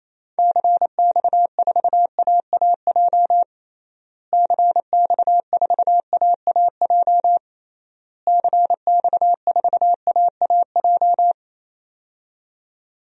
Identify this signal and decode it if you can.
Morse code, heard as a keyed tone, about 28 wpm, 700 Hz. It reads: CX4AAJ CX4AAJ CX4AAJ
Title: DXCC-100 Call Signs - 6 Characters - Repeated 3x - 28wpm